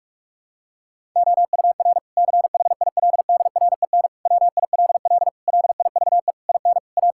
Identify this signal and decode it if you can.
Morse code, heard as a keyed tone, about 46 wpm, 700 Hz. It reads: OUR CHILDREN WILL LIVE IN A
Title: Sentences from Top 1000 Words - Review Part 5 - 45wpm